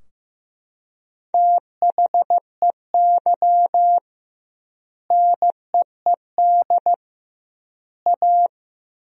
Morse code, heard as a keyed tone, about 15 wpm, 700 Hz. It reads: THEY NEED A
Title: 2x Word-Spacing - Sentences from Top 500 Words - 15wpm